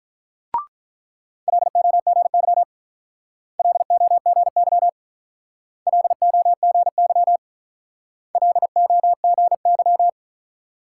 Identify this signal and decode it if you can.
Morse code, heard as a keyed tone, about 44 wpm, 700 Hz. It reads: LOGY LOGY LOGY LOGY
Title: Speed-Racing Multi-letter Phonemes - 35wpm